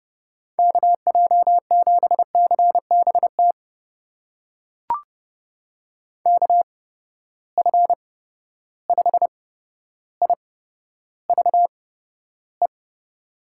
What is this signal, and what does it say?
Morse code, heard as a keyed tone, about 30 wpm, spaced as for 13 wpm, 700 Hz. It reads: KJ7CBT KF5IVE
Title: USA Call Signs - 6 Characters - 8x Character Spacing - 30wpm